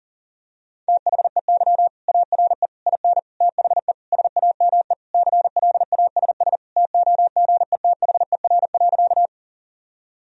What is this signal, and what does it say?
Morse code, heard as a keyed tone, about 40 wpm, 700 Hz. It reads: THEY ARE IN THE SAME CLASS TOGETHER.
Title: Sentences from Top 1000 Words - Review Part 2 - 40wpm